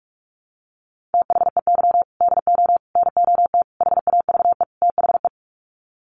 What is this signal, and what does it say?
Morse code, heard as a keyed tone, about 44 wpm, 700 Hz. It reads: THEY DO NOT HAVE THE
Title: Sets of 5 Words 45wpm